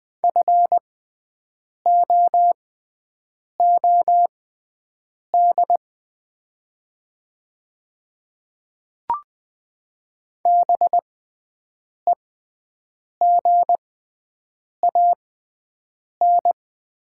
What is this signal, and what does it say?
Morse code, heard as a keyed tone, about 20 wpm, spaced as for 7 wpm, 700 Hz. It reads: FOOD BEGAN
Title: 6x Character Spacing - Top 300 Words - 20wpm